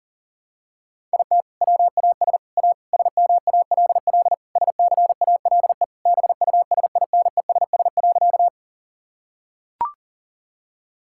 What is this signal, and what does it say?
Morse code, heard as a keyed tone, about 40 wpm, 700 Hz. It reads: IT WAS A SMALL SCALE BUSINESS.
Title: Sentences from Top 1000 Words - Review Part 5 - 40wpm